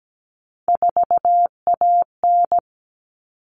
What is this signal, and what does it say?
Morse code, heard as a keyed tone, about 17 wpm, 700 Hz. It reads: 4AN
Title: All Cut Numbers - 3-Digit 17wpm